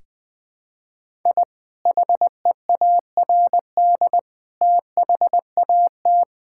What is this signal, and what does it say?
Morse code, heard as a keyed tone, about 21 wpm, 700 Hz. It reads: I HEARD THAT
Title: Sentences with 2-Word Phrases - 20wpm